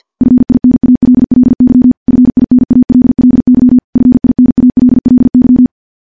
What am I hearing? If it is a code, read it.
Morse code, heard as a keyed tone, about 50 wpm, 250 Hz. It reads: FINALLY FINALLY FINALLY
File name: morse-training-50WPM